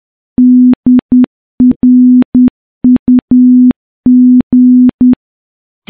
Morse code, heard as a keyed tone, about 10 wpm, 250 Hz. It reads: DRUG
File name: morse-training-10WPM